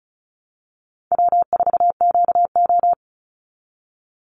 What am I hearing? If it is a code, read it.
Morse code, heard as a keyed tone, about 34 wpm, 700 Hz. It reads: W4QO
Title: Easy Call Signs Repeated 3x - 35wpm